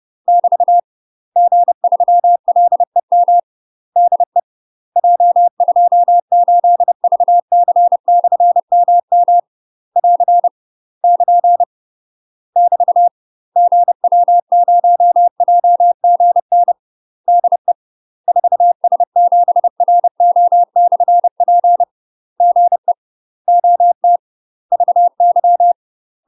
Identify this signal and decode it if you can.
Morse code, heard as a keyed tone, about 30 wpm, 700 Hz. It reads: <BT> G3LEM DE J28VC/MM <AR> <KN> <BT> GW0JGN DE 4S7RO/P GE OT VY
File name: UK MASTER QSOs TRAINING / 30WPM / UK MASTER QSOs 0022